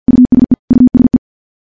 Morse code, heard as a keyed tone, about 50 wpm, 250 Hz. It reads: USE USE
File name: morse / morse-training-50WPM